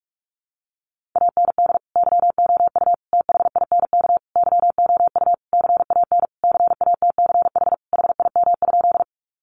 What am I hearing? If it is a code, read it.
Morse code, heard as a keyed tone, about 44 wpm, 700 Hz. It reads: AND YOU THINK YOU CAN CATCH HIM?
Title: Sentences from Top 1500 Words 45wpm